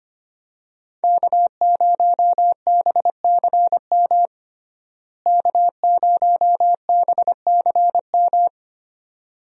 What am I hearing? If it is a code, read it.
Morse code, heard as a keyed tone, about 25 wpm, 700 Hz. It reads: K0BCM K0BCM
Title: USA Call Signs - 5 Characters - Repeated 2x - 25wpm